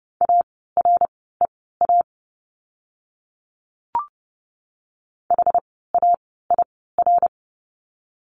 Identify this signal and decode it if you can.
Morse code, heard as a keyed tone, about 30 wpm, spaced as for 17 wpm, 700 Hz. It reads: AREA HAIR
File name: Lesson 17f - Letter u - Words 30wpm